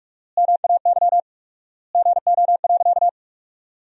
Morse code, heard as a keyed tone, about 46 wpm, 700 Hz. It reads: MAY GO.
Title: 4x Word-Spacing - Sentences from Top 100 Words - 45wpm